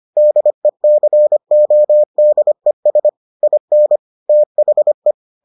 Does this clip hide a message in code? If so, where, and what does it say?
Yes: Morse code from 0.2 to 5.1 s, heard as a keyed tone, about 25 wpm, 600 Hz. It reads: DECODES IN THE